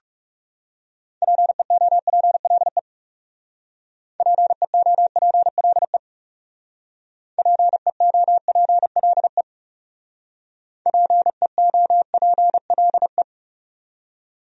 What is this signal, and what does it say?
Morse code, heard as a keyed tone, about 37 wpm, 700 Hz. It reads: PEOPLE PEOPLE PEOPLE PEOPLE
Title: Speed-Racing Top 100 Words 30wpm